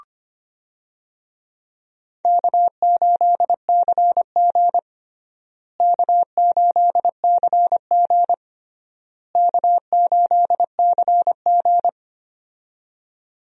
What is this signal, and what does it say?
Morse code, heard as a keyed tone, about 25 wpm, 700 Hz. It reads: K8CG K8CG K8CG
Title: USA Call Signs - 4 Characters - Repeated 3x - 25wpm